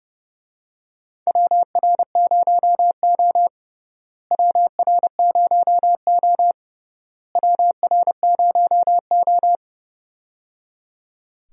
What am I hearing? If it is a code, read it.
Morse code, heard as a keyed tone, about 30 wpm, 700 Hz. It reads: WR0O WR0O WR0O
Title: USA Call Signs - 4 Characters - Repeated 3x - 30wpm